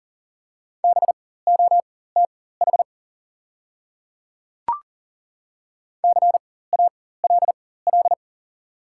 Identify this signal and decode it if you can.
Morse code, heard as a keyed tone, about 40 wpm, spaced as for 19 wpm, 700 Hz. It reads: BOTH CALL
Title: 4x Character Spacing - Top 300 Words - 40wpm